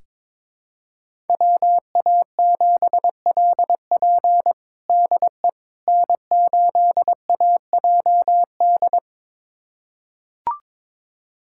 Morse code, heard as a keyed tone, about 22 wpm, 700 Hz. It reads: WA7LP DE N8AJD
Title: USA Callsign DE Callsign - 5 Characters - 22wpm